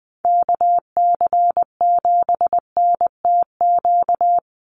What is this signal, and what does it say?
Morse code, heard as a keyed tone, about 20 wpm, 700 Hz. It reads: KC7NTQ
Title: USA Call Signs 20wpm